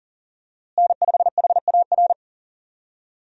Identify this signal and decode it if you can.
Morse code, heard as a keyed tone, about 40 wpm, 700 Hz. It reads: N5HAR
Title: USA Call Signs - 5 Characters - Repeated 3x - 40wpm